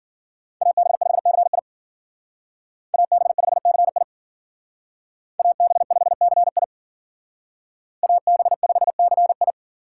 Morse code, heard as a keyed tone, about 52 wpm, 700 Hz. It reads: AB5CI AB5CI AB5CI AB5CI